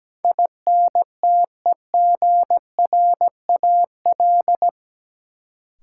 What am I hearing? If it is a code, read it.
Morse code, heard as a keyed tone, about 17 wpm, 700 Hz. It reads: INTEGRAL
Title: Sets of 5 Words - New Word Drill - 17wpm